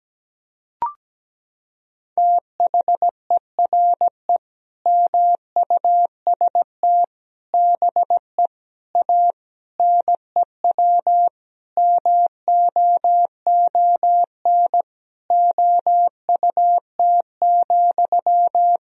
Morse code, heard as a keyed tone, about 17 wpm, 700 Hz. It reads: THERE MUST BE A NEW MOON OUT,
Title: Sentences from Top 1000 Words - Review Part 3 - 17wpm